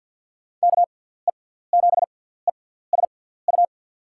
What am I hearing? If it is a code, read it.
Morse code, heard as a keyed tone, about 47 wpm, spaced as for 17 wpm, 700 Hz. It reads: KE7ESU
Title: USA Call Signs - 6 Characters - 6x Character Spacing - 50wpm